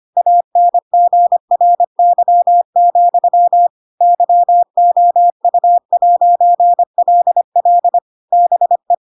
Morse code, heard as a keyed tone, about 25 wpm, 700 Hz. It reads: ANGRY, YOU'LL BE
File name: Pinocchio / Pinocchio ch4 25 25 ik1hge